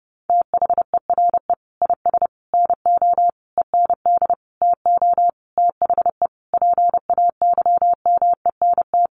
Morse code, heard as a keyed tone, about 30 wpm, 700 Hz. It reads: THERE IS NO END TO THE PAYMENT
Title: Sentences from Top 1200 Words 30wpm